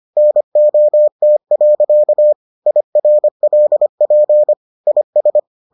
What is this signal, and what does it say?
Morse code, heard as a keyed tone, about 25 wpm, 600 Hz. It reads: NOT. IRLP IS